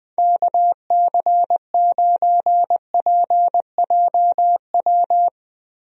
Morse code, heard as a keyed tone, about 20 wpm, 700 Hz. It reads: KC9PJW
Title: USA Call Signs - 6 Characters - Repeated 3x - 20wpm